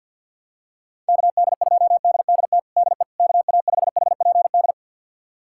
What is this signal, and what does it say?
Morse code, heard as a keyed tone, about 51 wpm, 700 Hz. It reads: KD2DDT DE KA5HPD
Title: USA Callsign DE Callsign - 6 Characters - 50wpm